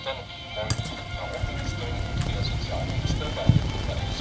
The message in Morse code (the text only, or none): none